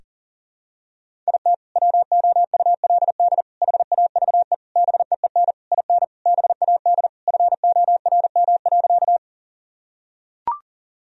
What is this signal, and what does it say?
Morse code, heard as a keyed tone, about 40 wpm, 700 Hz. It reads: IT WOULD HAVE BEEN IN BAD FORM.
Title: Sentences from Top 1000 Words - Review Part 1 - 40wpm